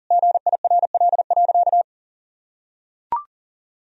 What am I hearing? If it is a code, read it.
Morse code, heard as a keyed tone, about 40 wpm, 700 Hz. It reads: GIRL.
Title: Sentences from Top 400 Words and Abbreviations 40wpm